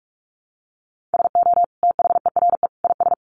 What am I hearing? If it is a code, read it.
Morse code, heard as a keyed tone, about 44 wpm, 700 Hz. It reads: SO THERE IS
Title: Sentences from Top 1500 Words 45wpm